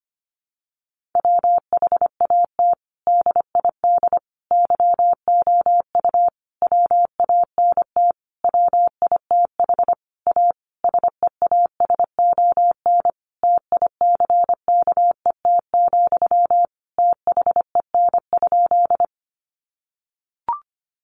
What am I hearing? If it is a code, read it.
Morse code, heard as a keyed tone, about 25 wpm, 700 Hz. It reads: WHAT DID YOU WANT WITH A SEASON TICKET, THEN?
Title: Sentences from Top 1400 Words 25wpm